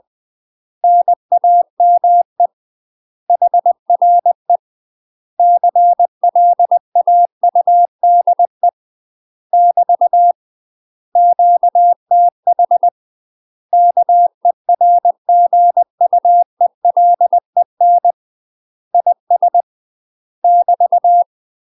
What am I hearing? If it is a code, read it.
Morse code, heard as a keyed tone, about 20 wpm, 700 Hz. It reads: NAME HRE CLAUDE <BT> QTH KERGUELEN IS <BT>
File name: UK MASTER QSOs TRAINING / 20WPM / UK MASTER QSOs 0028